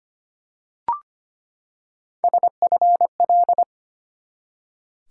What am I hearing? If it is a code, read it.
Morse code, heard as a keyed tone, about 25 wpm, 700 Hz. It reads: SFL
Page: morse-code-ninja-practice-25wpm